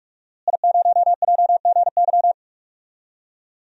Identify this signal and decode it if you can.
Morse code, heard as a keyed tone, about 43 wpm, 700 Hz. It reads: I0JGY
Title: DXCC-100 Call Signs - 5 Characters - Repeated 2x - 45wpm